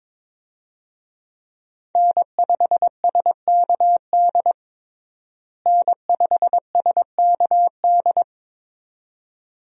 Morse code, heard as a keyed tone, about 21 wpm, 700 Hz. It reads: N5SKD N5SKD
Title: USA Call Signs - 5 Characters - Repeated 2x - 22wpm